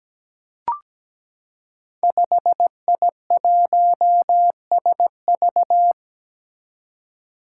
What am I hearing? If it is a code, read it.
Morse code, heard as a keyed tone, about 17 wpm, 700 Hz. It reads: 5I1SV
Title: Five Challenging Letter-Number Combinations 17wpm